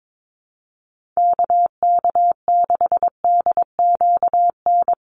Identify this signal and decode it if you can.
Morse code, heard as a keyed tone, about 22 wpm, 700 Hz. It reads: KK6DQN
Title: USA Call Signs 22wpm